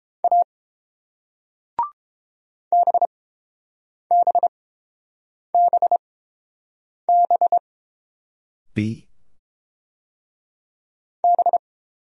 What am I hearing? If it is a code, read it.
Morse code, heard as a keyed tone, about 30 wpm, spaced as for 7 wpm, 690 Hz. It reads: A BBBB B